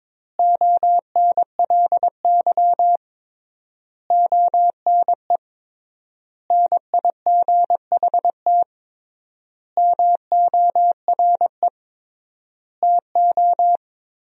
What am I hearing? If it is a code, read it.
Morse code, heard as a keyed tone, about 22 wpm, 700 Hz. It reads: ONLY ONE NIGHT MORE TO
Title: 3x Word-Spacing - Sentences from Top 300 Words - 22wpm